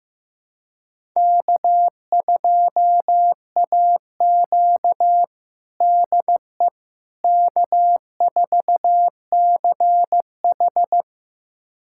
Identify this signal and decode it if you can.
Morse code, heard as a keyed tone, about 15 wpm, 700 Hz. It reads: K2AQ DE K4CH
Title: USA Callsign DE Callsign - 4 Characters - 15wpm